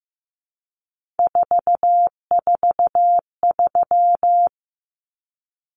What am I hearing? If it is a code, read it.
Morse code, heard as a keyed tone, about 15 wpm, 700 Hz. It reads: 443
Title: Numbers - 3-Digit 15wpm